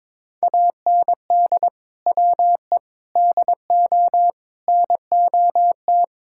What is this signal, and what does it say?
Morse code, heard as a keyed tone, about 21 wpm, 700 Hz. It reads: AND WE DO NOT
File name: practice / 22wpm / Sentences with 4-Word Phrases - 22wpm